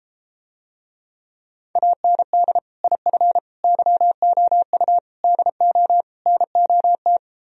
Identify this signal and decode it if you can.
Morse code, heard as a keyed tone, about 33 wpm, 700 Hz. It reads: AND IF YOU DO NOT